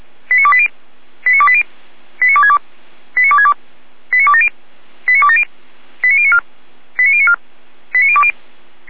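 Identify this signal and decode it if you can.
Five-tone selective call (ZVEI), heard as quick runs of five tones, tones of about 70 ms. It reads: 89280 89280 89262 89262 89270 89270 89094 89094 89020
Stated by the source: ZVEI five-tone selective call 89304 received by SDR.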